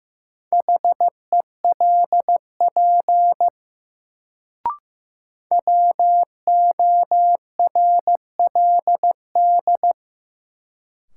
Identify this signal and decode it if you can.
Morse code, heard as a keyed tone, about 15 wpm, 700 Hz. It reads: HELP WORLD
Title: Sentences from Top 200 Words - New Word Drill - 15wpm